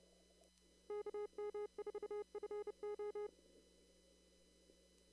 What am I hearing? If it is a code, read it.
Morse code, heard as a keyed tone, about 29 wpm, 420 Hz. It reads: KM4FO
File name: CW / kt5j wpx cw-op-k5pi